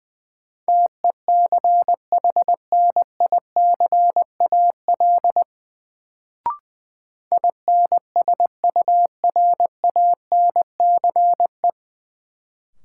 Morse code, heard as a keyed tone, about 20 wpm, 700 Hz. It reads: TECHNICAL INSURANCE
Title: Sentences from Top 1100 Words - New Word Drill Alt - 20wpm